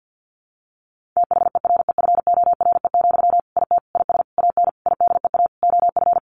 Morse code, heard as a keyed tone, about 50 wpm, 700 Hz. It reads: THEREFORE, IT IS AN IDEA OF
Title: Sentences from Top 1500 Words 50wpm